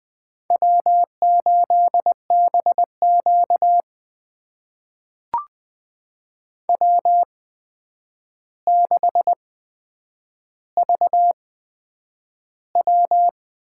Morse code, heard as a keyed tone, about 20 wpm, spaced as for 9 wpm, 700 Hz. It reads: W8BQ W6VW
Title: USA Call Signs - 8x Character Spacing - 20wpm